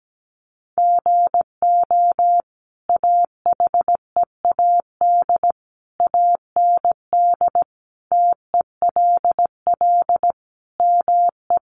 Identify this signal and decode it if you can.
Morse code, heard as a keyed tone, about 17 wpm, 700 Hz. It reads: GO AHEAD AND TELL ME
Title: Sentences from Top 1100 Words 17wpm